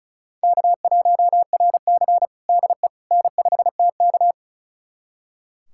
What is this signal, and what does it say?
Morse code, heard as a keyed tone, about 34 wpm, 700 Hz. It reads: K1RC DE N5TK
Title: USA Callsign DE Callsign - 4 Characters - 35wpm